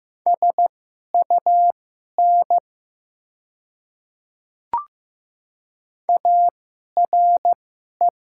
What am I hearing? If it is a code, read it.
Morse code, heard as a keyed tone, about 15 wpm, spaced as for 11 wpm, 700 Hz. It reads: SUN ARE
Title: 2x Character Spacing - Top 500 Words - 15wpm